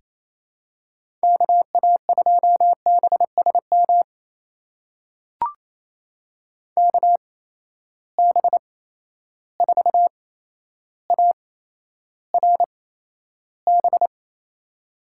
Morse code, heard as a keyed tone, about 28 wpm, spaced as for 12 wpm, 700 Hz. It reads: KA2BSM KB4ARB
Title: USA Call Signs - 6 Characters - 8x Character Spacing - 28wpm